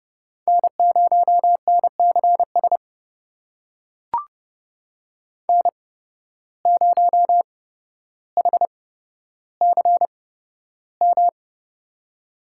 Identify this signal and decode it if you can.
Morse code, heard as a keyed tone, about 30 wpm, spaced as for 13 wpm, 700 Hz. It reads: N0NCS N0HCM
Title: USA Call Signs - 5 Characters - 8x Character Spacing - 30wpm